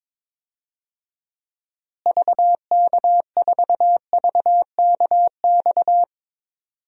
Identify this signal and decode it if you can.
Morse code, heard as a keyed tone, about 22 wpm, 700 Hz. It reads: VK4VKX
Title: DXCC-100 Call Signs - 6 Characters - Repeated 2x - 22wpm